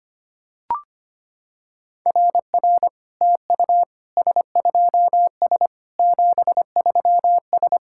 Morse code, heard as a keyed tone, about 25 wpm, 690 Hz. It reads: RR TU S2S 73S
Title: SOTA QSO Practice 25wpm